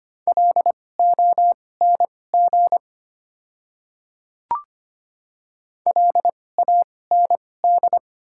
Morse code, heard as a keyed tone, about 25 wpm, spaced as for 18 wpm, 700 Hz. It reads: LONG LAND
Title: 2x Character Spacing - Top 200 Words - 25wpm